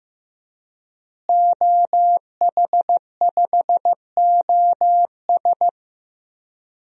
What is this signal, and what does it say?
Morse code, heard as a keyed tone, about 16 wpm, 700 Hz. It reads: OH5OS
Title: DXCC-100 Call Signs - 5 Characters - Repeated 2x - 15wpm